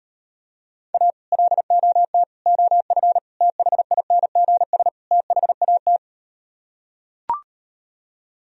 Morse code, heard as a keyed tone, about 39 wpm, 700 Hz. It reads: A LOT OF THINGS THAT